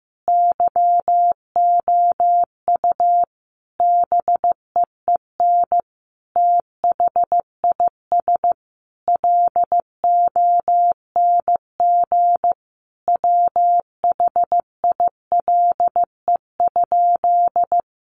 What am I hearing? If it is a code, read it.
Morse code, heard as a keyed tone, about 14 wpm, 700 Hz. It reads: YOU BEEN THIS LONG WHILE?
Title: Sentences from Top 300 Words 15wpm